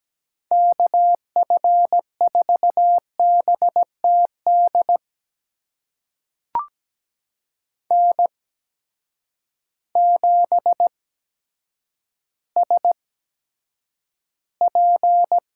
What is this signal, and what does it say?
Morse code, heard as a keyed tone, about 17 wpm, spaced as for 8 wpm, 700 Hz. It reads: KF4BTD N7SP
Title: USA Call Signs - 8x Character Spacing - 17wpm